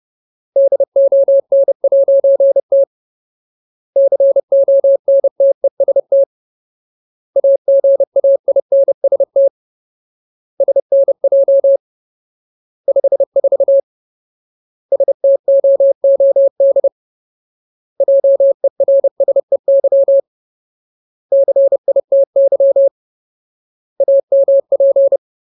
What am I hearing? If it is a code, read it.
Morse code, heard as a keyed tone, about 30 wpm, 550 Hz. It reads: DON'T CONTEST AGAINST SNJ 54 STOOD JERSEY CITY AMP
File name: Ultimate Word List / 30WPM Ultimate